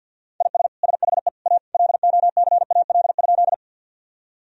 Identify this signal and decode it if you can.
Morse code, heard as a keyed tone, about 51 wpm, 700 Hz. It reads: IS SHE A LOCAL?
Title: Sentences from Top 1000 Words - Review Part 3 - 50wpm